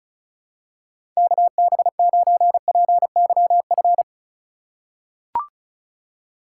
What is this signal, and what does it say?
Morse code, heard as a keyed tone, about 34 wpm, 700 Hz. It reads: KB9PYF
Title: USA Call Signs - 6 Characters - 2x Character Spacing - 35wpm